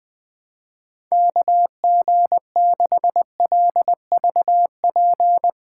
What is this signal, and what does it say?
Morse code, heard as a keyed tone, about 20 wpm, 700 Hz. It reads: KG6LVP
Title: Speed Racing - USA 2x3 Call Signs 17wpm